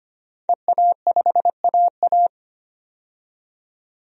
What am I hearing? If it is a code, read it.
Morse code, heard as a keyed tone, about 25 wpm, 700 Hz. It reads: EA5AA
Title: DXCC-100 Call Signs - 5 Characters - 25wpm